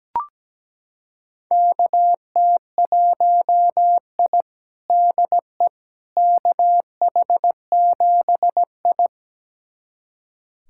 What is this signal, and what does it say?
Morse code, heard as a keyed tone, about 17 wpm, 700 Hz. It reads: KT1I DE KH7I
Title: USA Callsign DE Callsign - 4 Characters - 17wpm